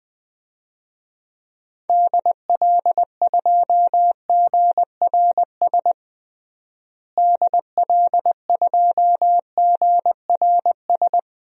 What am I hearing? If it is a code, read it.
Morse code, heard as a keyed tone, about 21 wpm, 700 Hz. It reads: DL2GRS DL2GRS
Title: DXCC-100 Call Signs - 6 Characters - Repeated 3x - 20wpm